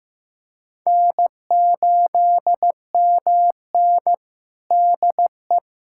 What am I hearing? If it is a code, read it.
Morse code, heard as a keyed tone, about 15 wpm, 700 Hz. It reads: N8MN DE
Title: USA Callsign DE Callsign 15wpm